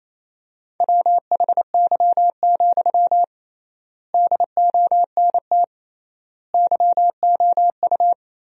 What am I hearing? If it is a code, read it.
Morse code, heard as a keyed tone, about 27 wpm, 700 Hz. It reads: WHY, DONT YOU
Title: 3x Word-Spacing - Sentences from Top 300 Words - 28wpm